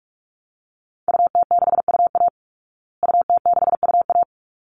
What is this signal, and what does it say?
Morse code, heard as a keyed tone, about 44 wpm, 700 Hz. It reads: UT6UA UT6UA
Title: Call Signs Repeated 3x - 45wpm